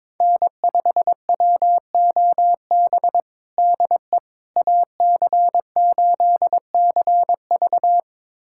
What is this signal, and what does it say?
Morse code, heard as a keyed tone, about 22 wpm, 700 Hz. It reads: N5WOB DE AC8CV
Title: USA Callsign DE Callsign - 5 Characters - 22wpm